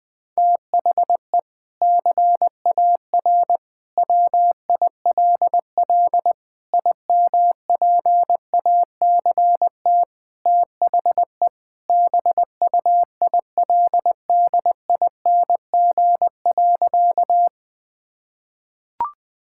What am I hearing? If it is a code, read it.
Morse code, heard as a keyed tone, about 21 wpm, 700 Hz. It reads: THE CAR WILL IMPACT THE BUILDING.
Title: Sentences from Top 1000 Words - Review Part 6 - 20wpm